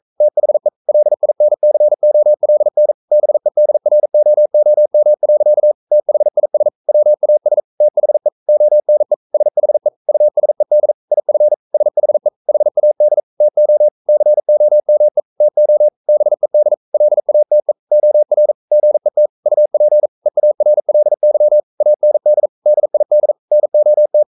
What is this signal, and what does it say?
Morse code, heard as a keyed tone, about 42 wpm, 600 Hz. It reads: THE LINCOLN BEDROOM. THIS WAS THE ONE SHE USED IF SHE HAD TO COME TO BED LATE OR GET UP EARLY AND DID NOT
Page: Romanen Never af Ken Follet på engelsk.